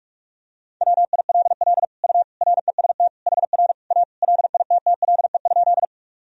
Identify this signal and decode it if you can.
Morse code, heard as a keyed tone, about 45 wpm, 700 Hz. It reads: WILL U REST HR A LITTLE?
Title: Sentences from Top 500 Words and Abbreviations 45wpm